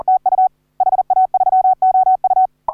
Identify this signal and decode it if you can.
Morse code, heard as a keyed tone, about 40 wpm, 750 Hz. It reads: TU HA3OU